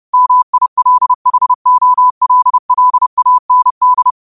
Morse code, heard as a keyed tone, about 30 wpm, 1000 Hz. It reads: MILHOLLAND